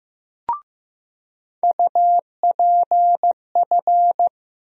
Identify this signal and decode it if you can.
Morse code, heard as a keyed tone, about 15 wpm, 700 Hz. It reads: UPF
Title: USA Call Sign Postfixes - 3 Letters 15wpm